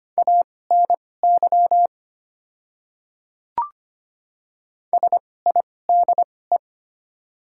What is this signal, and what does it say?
Morse code, heard as a keyed tone, about 25 wpm, spaced as for 18 wpm, 700 Hz. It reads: ANY SIDE
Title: 2x Character Spacing - Top 200 Words - 25wpm